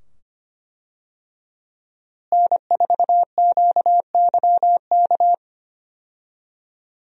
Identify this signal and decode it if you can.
Morse code, heard as a keyed tone, about 25 wpm, 700 Hz. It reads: N4QYK
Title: USA Call Signs - 5 Characters - Repeated 2x - 25wpm